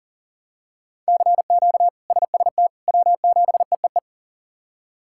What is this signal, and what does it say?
Morse code, heard as a keyed tone, about 40 wpm, 700 Hz. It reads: CQ SST W7EEE
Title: K1USN SST QSO Practice - 40wpm